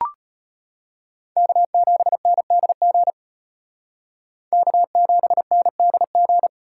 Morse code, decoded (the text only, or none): K7NDG K7NDG